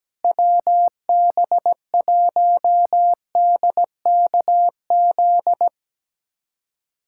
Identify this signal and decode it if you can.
Morse code, heard as a keyed tone, about 17 wpm, 700 Hz. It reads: WB1DKZ
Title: USA Callsign DE Callsign 17wpm